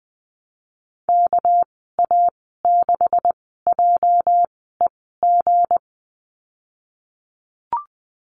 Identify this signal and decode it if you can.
Morse code, heard as a keyed tone, about 20 wpm, spaced as for 15 wpm, 700 Hz. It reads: KA6JEG
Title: 219 Lesson 35g - Number 0 - Call Signs 20wpm